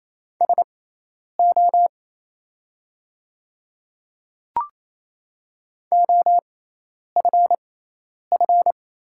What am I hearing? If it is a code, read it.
Morse code, heard as a keyed tone, about 28 wpm, spaced as for 10 wpm, 700 Hz. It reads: SO OFF
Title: 6x Character Spacing - Top 200 Words - 28wpm